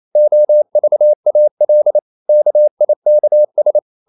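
Morse code, heard as a keyed tone, about 28 wpm, 600 Hz. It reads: OVAL KIKS
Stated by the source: Danish short words sent in Morse at 28 wpm, from CW med Gnister.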